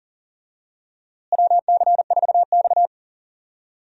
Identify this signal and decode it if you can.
Morse code, heard as a keyed tone, about 40 wpm, 700 Hz. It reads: WC4X
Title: USA Call Signs - 4 Characters - Repeated 2x - 40wpm